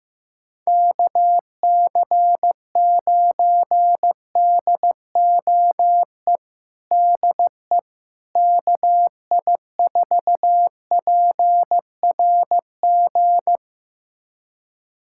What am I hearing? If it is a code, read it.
Morse code, heard as a keyed tone, about 15 wpm, 700 Hz. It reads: KC9DOE DE KI4PRG
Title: USA Callsign DE Callsign - 6 Characters - 15wpm